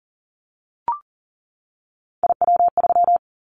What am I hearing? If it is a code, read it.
Morse code, heard as a keyed tone, about 40 wpm, 700 Hz. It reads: IW3
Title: Callsign Prefixes for Top 50 DXCC - 40wpm